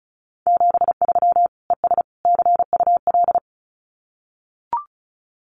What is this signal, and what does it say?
Morse code, heard as a keyed tone, about 34 wpm, 700 Hz. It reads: 73 ES CUL
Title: Common QSO Phrases Encore 35wpm